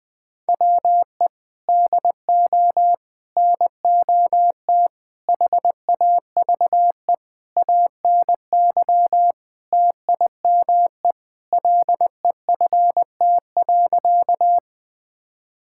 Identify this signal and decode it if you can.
Morse code, heard as a keyed tone, about 21 wpm, 700 Hz. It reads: WE DO NOT HAVE ANY TIME LEFT.
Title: Sentences with 4-Word Phrases - 20wpm